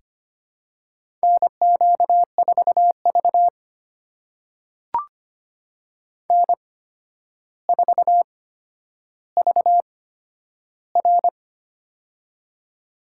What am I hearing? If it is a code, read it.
Morse code, heard as a keyed tone, about 25 wpm, spaced as for 11 wpm, 700 Hz. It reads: NQ4V N4VR
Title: USA Call Signs - 4 Characters - 8x Character Spacing - 25wpm